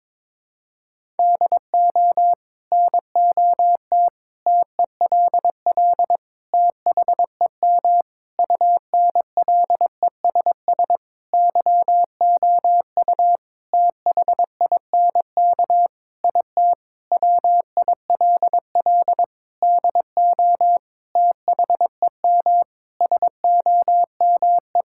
Morse code, heard as a keyed tone, about 22 wpm, 700 Hz. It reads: DO NOT TELL THEM UNLESS YOU THINK IT WILL DO THEM SOME
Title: Sentences from Top 1000 Words - Review Part 6 - 22wpm